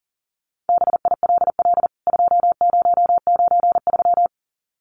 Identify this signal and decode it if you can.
Morse code, heard as a keyed tone, about 40 wpm, 700 Hz. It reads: BILL 2093
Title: CWT Exchange 40wpm